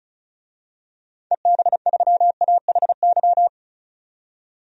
Morse code, heard as a keyed tone, about 36 wpm, 700 Hz. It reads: EB3AHY
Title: DXCC-100 Call Signs - 6 Characters - 35wpm